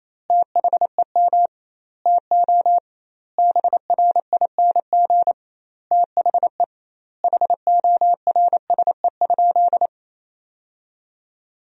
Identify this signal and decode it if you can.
Morse code, heard as a keyed tone, about 28 wpm, 700 Hz. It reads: THEM TO BRING THE HORSE?
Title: 2x Word-Spacing - Sentences from Top 500 Words - 28wpm